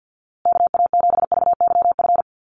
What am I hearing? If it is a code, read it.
Morse code, heard as a keyed tone, about 50 wpm, 700 Hz. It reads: KA7VYF
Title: USA Call Signs Repeated 3x 50wpm